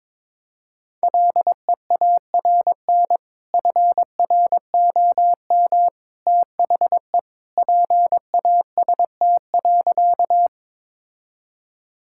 Morse code, heard as a keyed tone, about 22 wpm, 700 Hz. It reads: LEARN FROM THE PAST.
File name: practice / 22wpm / Sentences with 2-Word Phrases - 22wpm